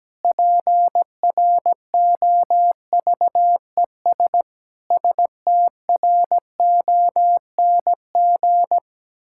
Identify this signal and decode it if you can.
Morse code, heard as a keyed tone, about 17 wpm, 700 Hz. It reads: PROVES STRONG
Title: Pangram Challenge v2 - 17wpm